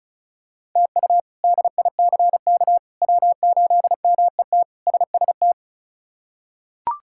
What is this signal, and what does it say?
Morse code, heard as a keyed tone, about 36 wpm, 700 Hz. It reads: TU DICK W8MET SST
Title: K1USN SST QSO Practice - 35wpm